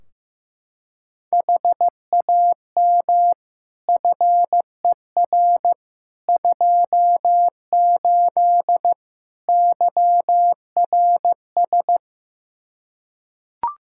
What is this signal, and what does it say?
Morse code, heard as a keyed tone, about 15 wpm, 700 Hz. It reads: HAM FER 28 YRS
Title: Practice QSO - Exchange 3 - More Info - 15wpm